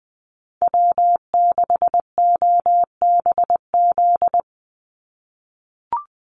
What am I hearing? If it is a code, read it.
Morse code, heard as a keyed tone, about 20 wpm, 700 Hz. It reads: W6OBZ
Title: USA Call Signs 20wpm